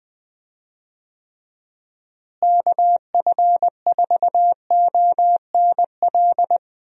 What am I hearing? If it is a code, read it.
Morse code, heard as a keyed tone, about 20 wpm, 700 Hz. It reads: KF4ONL
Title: USA Call Signs - 6 Characters - Repeated 2x - 20wpm